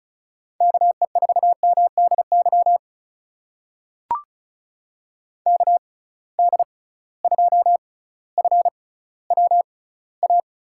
Morse code, heard as a keyed tone, about 36 wpm, spaced as for 18 wpm, 700 Hz. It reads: KE4MDY KD2FWA
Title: USA Call Signs - 6 Characters - 6x Character Spacing - 35wpm